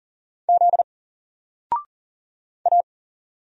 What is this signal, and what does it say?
Morse code, heard as a keyed tone, about 39 wpm, 700 Hz. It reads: Z A
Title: Single Character (Letter:Number:Punctuation:Prosign) 40wpm